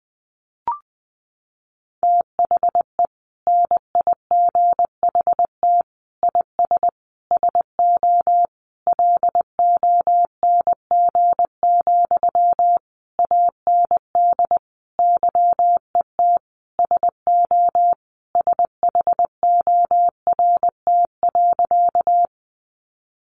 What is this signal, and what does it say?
Morse code, heard as a keyed tone, about 20 wpm, 700 Hz. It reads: THE NIGHT IS SO LONG, AND YET SO SHORT.
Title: Sentences from Top 500 Words 20wpm